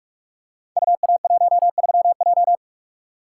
Morse code, heard as a keyed tone, about 44 wpm, 700 Hz. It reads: UA13J
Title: Code Groups - 5 Letter:Numbers - 45wpm